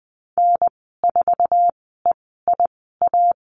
Morse code, heard as a keyed tone, about 20 wpm, spaced as for 14 wpm, 700 Hz. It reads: N4EIA
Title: 048 Lesson 9g - Number 4 - Call Signs 20wpm